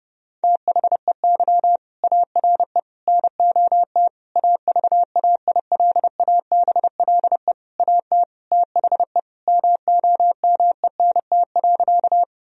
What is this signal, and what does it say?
Morse code, heard as a keyed tone, about 32 wpm, 700 Hz. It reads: THEY ARE NOT AVAILABLE AT THE MOMENT.
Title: Sentences from Top 1000 Words - Review Part 3 - 30wpm